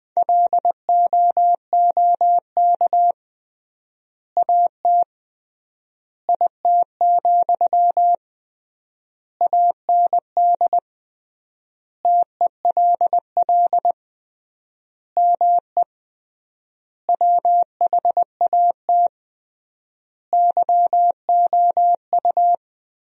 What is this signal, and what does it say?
Morse code, heard as a keyed tone, about 21 wpm, 700 Hz. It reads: LOOK AT IT, AND TELL ME WHAT YOU
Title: 3x Word-Spacing - Sentences from Top 200 Words - 20wpm